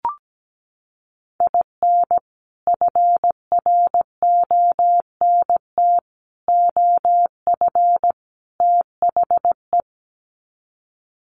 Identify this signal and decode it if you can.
Morse code, heard as a keyed tone, about 17 wpm, 700 Hz. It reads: IN FRONT OF THE
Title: Sets of 4 Words 17wpm